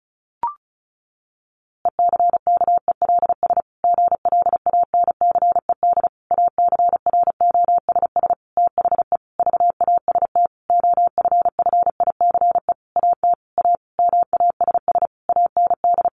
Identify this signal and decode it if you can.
Morse code, heard as a keyed tone, about 34 wpm, 700 Hz. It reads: ECKELS GLANCED ACROSS THE VAST OFFICE AT A MASS AND
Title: A Sound Of Thunder 35wpm